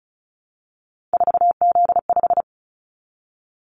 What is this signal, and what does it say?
Morse code, heard as a keyed tone, about 36 wpm, 700 Hz. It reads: 4Z5
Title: Callsign Prefixes for Top 50 DXCC - 35wpm